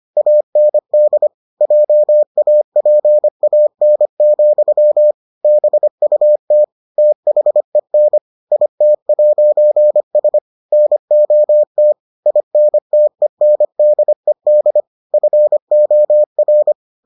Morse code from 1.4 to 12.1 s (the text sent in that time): JAPAN, BUT THEN IT'S NOT